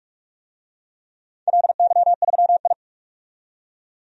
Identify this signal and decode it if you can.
Morse code, heard as a keyed tone, about 46 wpm, 700 Hz. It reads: LY3I
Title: DXCC-100 Call Signs - 4 Characters - 45wpm